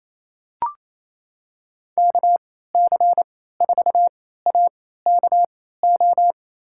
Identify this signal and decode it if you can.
Morse code, heard as a keyed tone, about 28 wpm, spaced as for 16 wpm, 700 Hz. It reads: KC4AKO
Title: USA Call Signs - 6 Characters - 3x Character Spacing - 28wpm